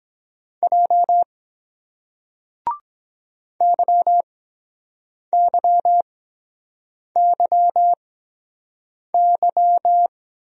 Morse code, heard as a keyed tone, about 23 wpm, spaced as for 7 wpm, 700 Hz. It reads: J YYYY